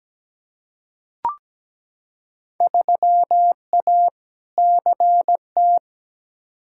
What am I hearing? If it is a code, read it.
Morse code, heard as a keyed tone, about 17 wpm, 700 Hz. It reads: 3A CT
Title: Field Day Exchanges - 17wpm